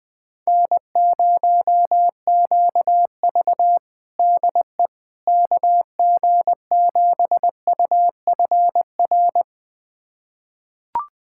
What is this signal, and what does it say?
Morse code, heard as a keyed tone, about 20 wpm, 700 Hz. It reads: N0QV DE KG7UFR
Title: USA Callsign DE Callsign 20wpm